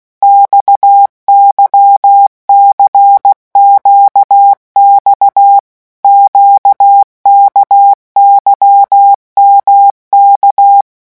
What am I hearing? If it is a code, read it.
Morse code, heard as a keyed tone, about 16 wpm, 800 Hz. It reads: XYCQX QKYMK
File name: morse mp3 / revision1-vitesse 15 mots